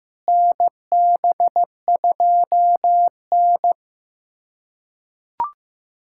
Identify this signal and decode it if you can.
Morse code, heard as a keyed tone, about 15 wpm, 700 Hz. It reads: NB2N
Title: USA Callsign DE Callsign - 4 Characters - 15wpm